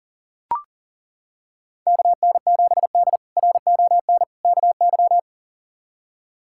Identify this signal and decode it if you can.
Morse code, heard as a keyed tone, about 40 wpm, 700 Hz. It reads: KN7D RON KY
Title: K1USN SST QSO Practice - 40wpm